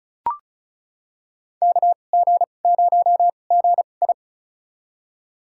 Morse code, heard as a keyed tone, about 36 wpm, spaced as for 25 wpm, 700 Hz. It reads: KG0GI
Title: USA Call Signs - 5 Characters - 2x Character Spacing - 35wpm